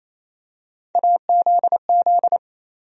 Morse code, heard as a keyed tone, about 28 wpm, 700 Hz. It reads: AZZ
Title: USA Call Sign Postfixes - 3 Letters 28wpm